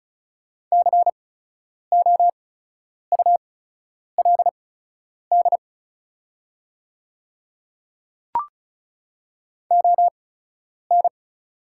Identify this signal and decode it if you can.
Morse code, heard as a keyed tone, about 36 wpm, spaced as for 10 wpm, 700 Hz. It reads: COULD ON
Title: 8x Character Spacing - Top 200 Words - 35wpm